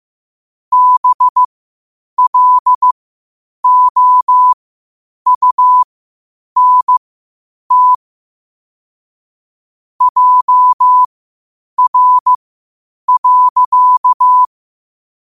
Morse code, heard as a keyed tone, about 15 wpm, spaced as for 8 wpm, 1000 Hz. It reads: BLOUNT JR.
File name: qotdfiles / 8wpm